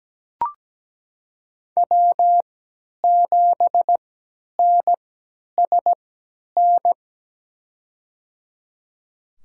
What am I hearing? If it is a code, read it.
Morse code, heard as a keyed tone, about 17 wpm, spaced as for 10 wpm, 700 Hz. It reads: W7NSN
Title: USA Call Signs - 5 Characters - 3x Character Spacing - 17wpm